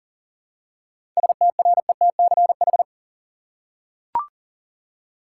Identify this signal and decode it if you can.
Morse code, heard as a keyed tone, about 40 wpm, 700 Hz. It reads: STRETCH
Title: Seven Letter Words - 40wpm